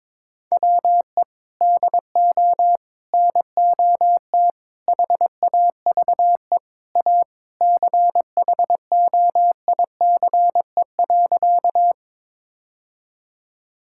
Morse code, heard as a keyed tone, about 22 wpm, 700 Hz. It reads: WE DO NOT HAVE A CHOICE.
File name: practice / 22wpm / Sentences with 3-Word Phrases - 22wpm